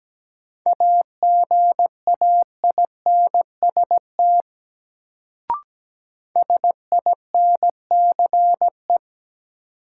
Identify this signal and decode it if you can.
Morse code, heard as a keyed tone, about 17 wpm, 700 Hz. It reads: AGAINST SINCE
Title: Sets of 2 Words - New Word Drill - 17wpm